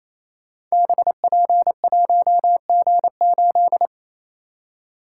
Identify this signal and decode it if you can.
Morse code, heard as a keyed tone, about 27 wpm, 700 Hz. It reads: BP1G8
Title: Code Groups - 5 Letter:Numbers - 28wpm